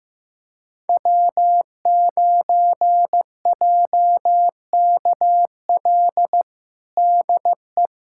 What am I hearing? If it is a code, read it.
Morse code, heard as a keyed tone, about 15 wpm, 700 Hz. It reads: W9JKL DE
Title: USA Callsign DE Callsign 15wpm